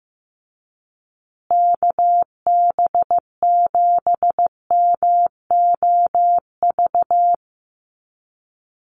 Morse code, heard as a keyed tone, about 15 wpm, 700 Hz. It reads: KB7MOV
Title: USA Call Signs Repeated 3x 15wpm